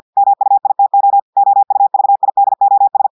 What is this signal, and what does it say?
Morse code, heard as a keyed tone, about 50 wpm, 800 Hz. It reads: MUITO OUVIDOS
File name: portugais texte en clair 2 vitesse 50 mots minutes